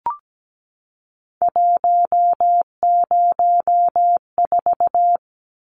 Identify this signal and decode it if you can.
Morse code, heard as a keyed tone, about 17 wpm, 700 Hz. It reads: 104
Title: Numbers - 3-Digit 17wpm